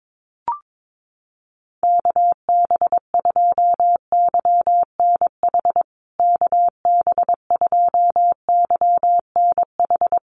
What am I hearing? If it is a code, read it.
Morse code, heard as a keyed tone, about 22 wpm, 700 Hz. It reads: KB2YNH KB2YNH
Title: USA Call Signs Repeated 3x 22wpm